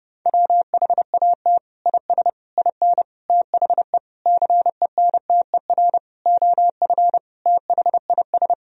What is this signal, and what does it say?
Morse code, heard as a keyed tone, about 30 wpm, 700 Hz. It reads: WHAT IS IN THE CENTER OF THIS
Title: Sentences from Top 1000 Words - Review Part 2 - 30wpm